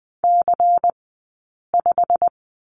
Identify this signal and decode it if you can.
Morse code, heard as a keyed tone, about 20 wpm, spaced as for 8 wpm, 700 Hz. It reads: C5
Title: 204 Lesson 33d - Letter j - Two Characters 20wpm